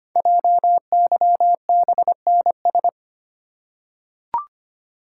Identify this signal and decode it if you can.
Morse code, heard as a keyed tone, about 25 wpm, 700 Hz. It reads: JYBNS
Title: Code Groups - 5 Letters - 25wpm